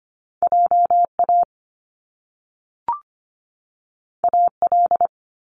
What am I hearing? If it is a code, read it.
Morse code, heard as a keyed tone, about 25 wpm, 700 Hz. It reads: JA AL
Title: Top Two Letter Combinations in Words 25wpm